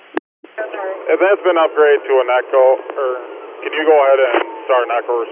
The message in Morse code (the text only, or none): none